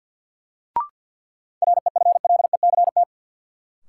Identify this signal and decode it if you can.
Morse code, heard as a keyed tone, about 47 wpm, 700 Hz. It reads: REFLECT
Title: Sentences from Top 1500 Words - New Word Drill - 50wpm